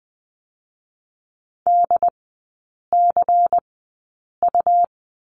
Morse code, heard as a keyed tone, about 20 wpm, 700 Hz. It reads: D C U
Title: 107 Lesson 18e - Letter m - Three Characters 20wpm